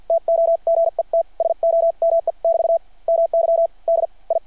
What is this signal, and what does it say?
Morse code, heard as a keyed tone, about 53 wpm, spaced as for 41 wpm, 670 Hz. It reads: TO GET SOME <BT> MY DI
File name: archived / CRSnet